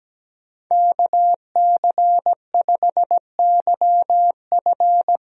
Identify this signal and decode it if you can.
Morse code, heard as a keyed tone, about 17 wpm, 700 Hz. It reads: KC5YF
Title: USA Call Signs - 5 Characters - Repeated 2x - 17wpm